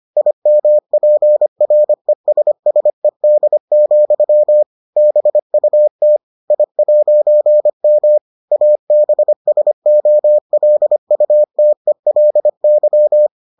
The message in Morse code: IMPRESSED, BUT I'M ABSOLUTELY

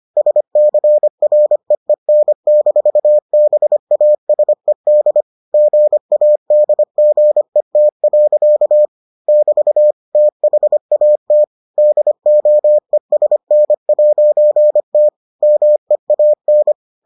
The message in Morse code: SCREEN-BASED GADGET. <BT> THAT DOESN'T MEAN

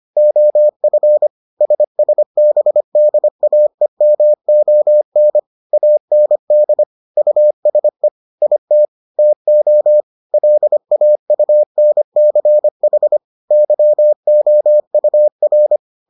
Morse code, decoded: OF SSBDAEMON AND USE IT TO LAUNCH YOUR

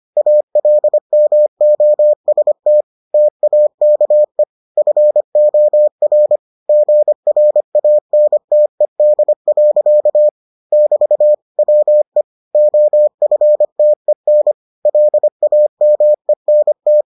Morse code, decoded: ALMOST TAKE FOR GRANTED. <BT> WE OFTEN LAMENT

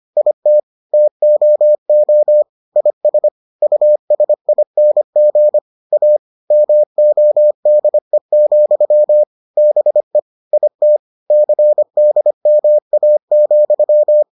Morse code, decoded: IT TOO IS USING A MODE, BE IT CDMA,

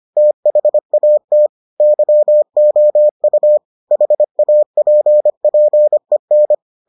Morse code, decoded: THAT YOU HAPPEN

